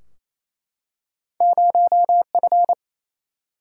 Morse code, heard as a keyed tone, about 28 wpm, 700 Hz. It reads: 0F
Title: Single-Letter Single-Number 28wpm